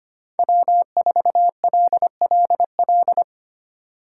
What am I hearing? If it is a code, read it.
Morse code, heard as a keyed tone, about 25 wpm, 700 Hz. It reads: W4LLL
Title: USA Call Signs - 5 Characters - 4x Character Spacing - 25wpm